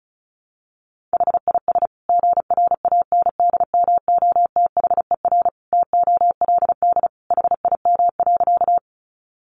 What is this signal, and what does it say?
Morse code, heard as a keyed tone, about 34 wpm, 700 Hz. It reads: HIS GRANDMOTHER TOLD HIM.
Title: Sentences from Top 1200 Words 35wpm